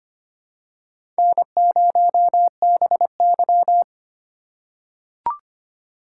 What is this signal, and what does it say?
Morse code, heard as a keyed tone, about 25 wpm, 700 Hz. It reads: N0BY
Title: USA Call Signs - 4 Characters - 4x Character Spacing - 25wpm